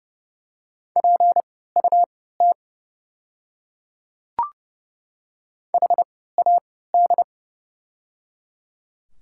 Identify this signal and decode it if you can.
Morse code, heard as a keyed tone, about 30 wpm, spaced as for 17 wpm, 700 Hz. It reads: PUT HAD
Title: 3x Character Spacing - Top 500 Words - 30wpm